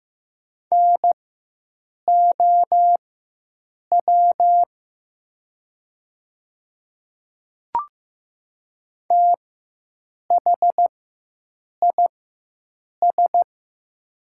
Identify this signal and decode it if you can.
Morse code, heard as a keyed tone, about 15 wpm, spaced as for 7 wpm, 700 Hz. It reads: NOW THIS
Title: 4x Character Spacing - Top 200 Words - 15wpm